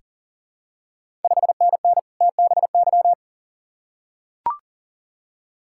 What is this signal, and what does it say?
Morse code, heard as a keyed tone, about 41 wpm, 690 Hz. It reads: 5NN TBY
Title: Ontario QSO Party Exchange 40wpm